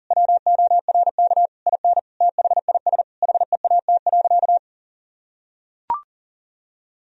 Morse code, heard as a keyed tone, about 40 wpm, 700 Hz. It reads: WORK IN THIS HEAT.
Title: Sentences from Top 1000 Words - Review Part 3 - 40wpm